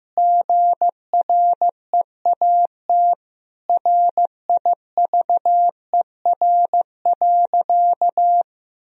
Morse code, heard as a keyed tone, about 15 wpm, 700 Hz. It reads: GREAT RIVER.
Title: Sentences from Top 1000 Words - Review Part 1 - 15wpm